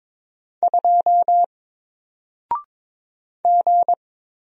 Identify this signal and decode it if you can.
Morse code, heard as a keyed tone, about 22 wpm, 700 Hz. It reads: 2 G
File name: Single-Letter-Number-22wpm